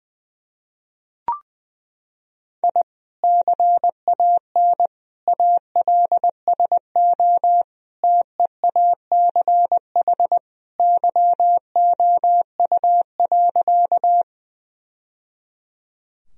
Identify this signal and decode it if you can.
Morse code, heard as a keyed tone, about 20 wpm, 700 Hz. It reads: I CAN ALSO TEACH YOU.
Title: Sentences from Top 1000 Words - Review Part 2 - 20wpm